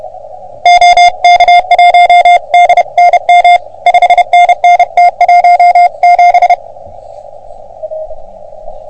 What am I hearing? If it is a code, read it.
Morse code, heard as a keyed tone, about 30 wpm, 720 Hz. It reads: OK1DNM 5NN T17
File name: OM3PA PILUP CUC 388